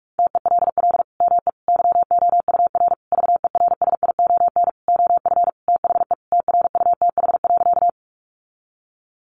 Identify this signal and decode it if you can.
Morse code, heard as a keyed tone, about 44 wpm, 700 Hz. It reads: TELL ME YOUR VERSION OF THE TRUTH.
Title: Sentences from Top 1400 Words 45wpm